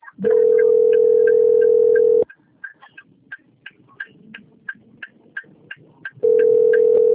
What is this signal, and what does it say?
Ringback tone (North America), 440 + 480 Hz.
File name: rg-1600-unknown- - - .57753